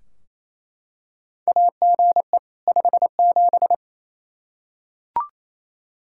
Morse code, heard as a keyed tone, about 28 wpm, 700 Hz. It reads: AGE 57
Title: Practice QSO - Exchange 3 - More Info - 28wpm